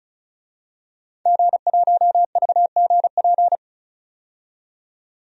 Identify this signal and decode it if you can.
Morse code, heard as a keyed tone, about 34 wpm, 700 Hz. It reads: G1VGP
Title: DXCC-100 Call Signs - 5 Characters - 35wpm